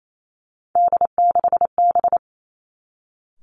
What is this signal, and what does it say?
Morse code, heard as a keyed tone, about 28 wpm, 700 Hz. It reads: D6B
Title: Three Challenging Letter-Number Combinations 28wpm